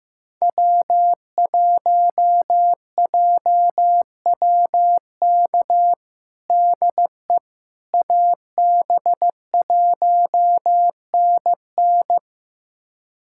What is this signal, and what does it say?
Morse code, heard as a keyed tone, about 15 wpm, 700 Hz. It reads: W1JWK DE AB1NN
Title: USA Callsign DE Callsign - 5 Characters - 15wpm